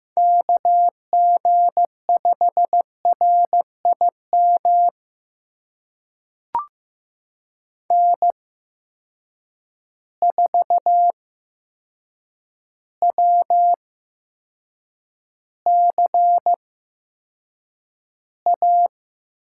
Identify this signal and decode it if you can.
Morse code, heard as a keyed tone, about 15 wpm, spaced as for 7 wpm, 700 Hz. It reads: KG5RIM N4WCA
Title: USA Call Signs - 8x Character Spacing - 15wpm